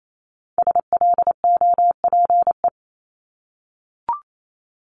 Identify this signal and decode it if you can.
Morse code, heard as a keyed tone, about 28 wpm, 700 Hz. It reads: SLOPE
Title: Multi-letter Phonemes Repeated 3x 28wpm